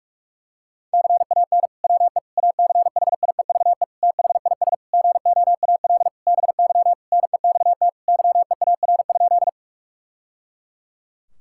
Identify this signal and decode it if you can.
Morse code, heard as a keyed tone, about 41 wpm, 700 Hz. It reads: CAN WE ACHIEVE THIS GOAL BY NEXT YEAR?
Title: Sentences from Top 1000 Words - Review Part 5 - 45wpm